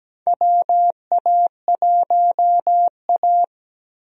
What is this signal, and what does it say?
Morse code, heard as a keyed tone, about 17 wpm, 700 Hz. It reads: WA1A
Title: USA Call Signs - 4 Characters - 17wpm